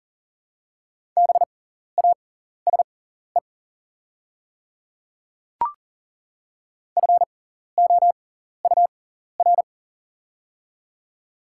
Morse code, heard as a keyed tone, about 40 wpm, spaced as for 14 wpm, 700 Hz. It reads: BASE FOUR
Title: 6x Character Spacing - Top 500 Words - 40wpm